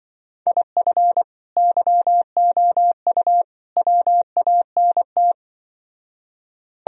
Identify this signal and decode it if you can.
Morse code, heard as a keyed tone, about 25 wpm, 700 Hz. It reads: IF YOU WANT
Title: Speed-Racing - Sets of 3 Words 20wpm